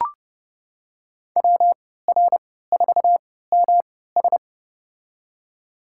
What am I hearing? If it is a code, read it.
Morse code, heard as a keyed tone, about 30 wpm, spaced as for 17 wpm, 700 Hz. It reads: WR4MS
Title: USA Call Signs - 5 Characters - 3x Character Spacing - 30wpm